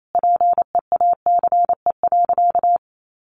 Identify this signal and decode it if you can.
Morse code, heard as a keyed tone, about 28 wpm, 700 Hz. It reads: PEACE.
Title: Sentences from Top 1500 Words 28wpm